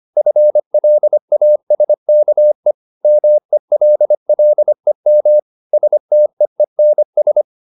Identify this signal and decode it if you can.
Morse code, heard as a keyed tone, about 25 wpm, 600 Hz. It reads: FLASKE MELLEM STEENS
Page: Ferie 25 wpm